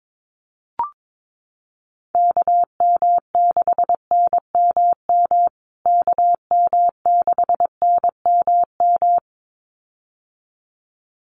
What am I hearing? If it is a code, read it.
Morse code, heard as a keyed tone, about 22 wpm, 700 Hz. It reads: KM6NMM KM6NMM
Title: USA Call Signs Repeated 2x 22wpm